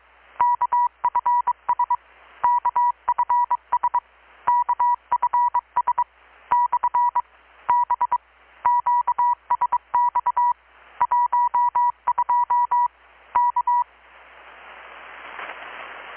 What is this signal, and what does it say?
Morse code, heard as a keyed tone, about 24 wpm, 970 Hz. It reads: KFS KFS KFS / B QSX 12 K
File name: blog / swl / 6-abril-2019-12695-USB-KPS